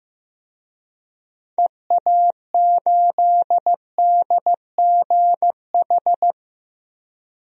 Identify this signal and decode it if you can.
Morse code, heard as a keyed tone, about 15 wpm, 700 Hz. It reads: EA8DGH
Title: DXCC-100 Call Signs - 6 Characters - 15wpm